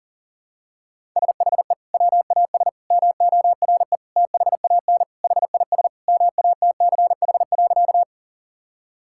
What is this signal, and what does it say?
Morse code, heard as a keyed tone, about 40 wpm, 700 Hz. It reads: SHE WAS MORE THAN HIS MATCH.
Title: Sentences from Top 1600 Words 40wpm